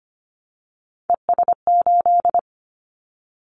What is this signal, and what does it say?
Morse code, heard as a keyed tone, about 25 wpm, 700 Hz. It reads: ES8
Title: Callsign Prefixes for Top 50 DXCC - 25wpm